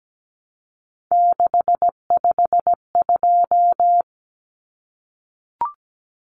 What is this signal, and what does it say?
Morse code, heard as a keyed tone, about 17 wpm, 700 Hz. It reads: 652
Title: Numbers - 3-Digit 17wpm